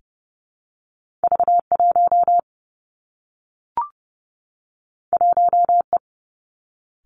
Morse code, heard as a keyed tone, about 31 wpm, 700 Hz. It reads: V1 1E
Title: Two Challenging Letter-Number Combinations 30wpm